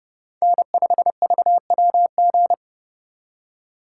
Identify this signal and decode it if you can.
Morse code, heard as a keyed tone, about 34 wpm, 700 Hz. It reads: N5VWG
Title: Speed Racing - USA Call Signs 30wpm